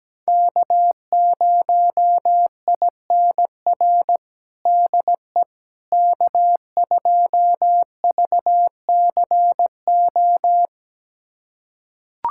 Morse code, heard as a keyed tone, about 17 wpm, 700 Hz. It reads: K0INR DE K2VCO
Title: USA Callsign DE Callsign - 5 Characters - 17wpm